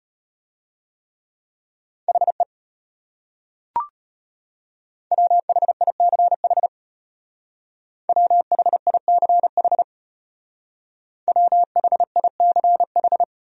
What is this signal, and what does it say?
Morse code, heard as a keyed tone, about 31 wpm, 700 Hz. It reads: HE WHICH WHICH WHICH
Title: Speed-Racing Top 100 Words 25wpm